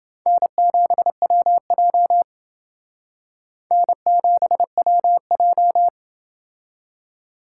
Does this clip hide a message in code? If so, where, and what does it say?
Yes: Morse code from 0.3 to 5.9 s, heard as a keyed tone, about 28 wpm, 700 Hz. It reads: N7WJ N7WJ